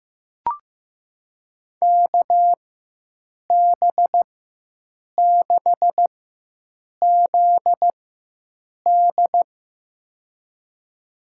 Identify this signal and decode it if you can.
Morse code, heard as a keyed tone, about 15 wpm, spaced as for 7 wpm, 700 Hz. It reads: KB6ZD
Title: USA Call Signs - 4x Character Spacing - 15wpm